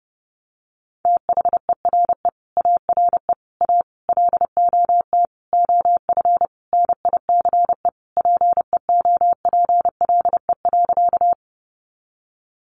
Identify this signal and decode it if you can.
Morse code, heard as a keyed tone, about 28 wpm, 700 Hz. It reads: THERE ARE A LOT OF NICE PEOPLE.
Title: Sentences from Top 1100 Words 30wpm